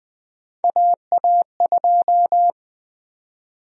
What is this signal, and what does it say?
Morse code, heard as a keyed tone, about 20 wpm, 700 Hz. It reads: AA2
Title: USA Call Sign Prefixes - Repeated 2x - 20wpm